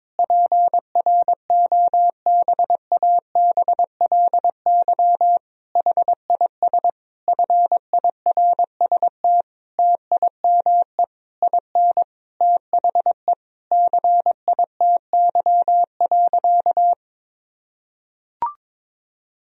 Morse code, heard as a keyed tone, about 22 wpm, 700 Hz. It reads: PROBABLY HIS FIRST TIME IN THE CITY.
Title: Sentences from Top 1000 Words - Review Part 4 - 22wpm